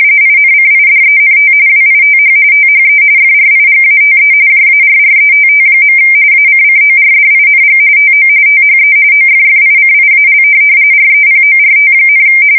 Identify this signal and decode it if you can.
RTTY, about 45.45 baud. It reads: INCIDENT WITH THE FRIGATE JARRETT. ACCORDING TO THE OFFICIAL REPORT,
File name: fsk170 / USS-Missouri